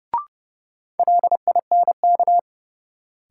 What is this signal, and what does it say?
Morse code, heard as a keyed tone, about 30 wpm, 700 Hz. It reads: LINK
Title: Sentences from Top 1500 Words - New Word Drill Alt - 30wpm